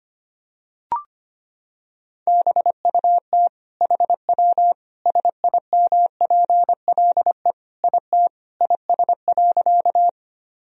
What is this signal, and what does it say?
Morse code, heard as a keyed tone, about 25 wpm, 700 Hz. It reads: BUT HW SIMPLE IT IS.
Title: Sentences from Top 400 Words and Abbreviations 25wpm